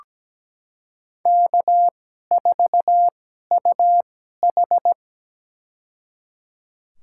Morse code, heard as a keyed tone, about 17 wpm, spaced as for 12 wpm, 700 Hz. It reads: K4UH
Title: USA Call Signs - 2x Character Spacing - 17wpm